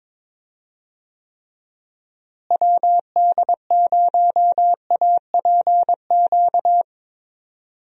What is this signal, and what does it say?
Morse code, heard as a keyed tone, about 22 wpm, 700 Hz. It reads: WD0APQ
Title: USA Call Signs - 6 Characters - Repeated 3x - 22wpm